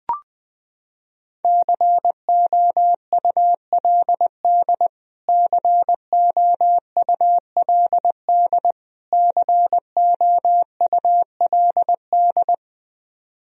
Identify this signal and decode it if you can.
Morse code, heard as a keyed tone, about 21 wpm, 700 Hz. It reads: COULD COULD COULD
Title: Top 100 Words Repeated 20wpm